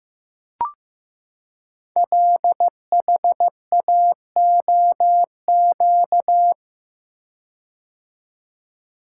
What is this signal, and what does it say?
Morse code, heard as a keyed tone, about 15 wpm, 700 Hz. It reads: LHAOQ
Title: Code Groups - 5 Letters - 15wpm